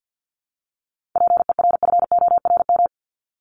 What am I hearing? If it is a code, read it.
Morse code, heard as a keyed tone, about 48 wpm, 700 Hz. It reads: PERFORM
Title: ED Suffix - Top 100 Words with Base Words 50wpm